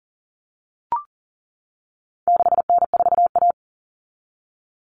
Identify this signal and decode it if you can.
Morse code, heard as a keyed tone, about 40 wpm, 700 Hz. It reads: 6N4A
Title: Common Cut Numbers - 4-Digit 40wpm